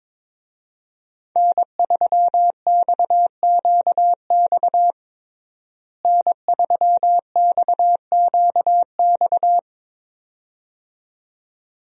Morse code, heard as a keyed tone, about 22 wpm, 700 Hz. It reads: N3XQX N3XQX
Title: USA Call Signs - 5 Characters - Repeated 2x - 22wpm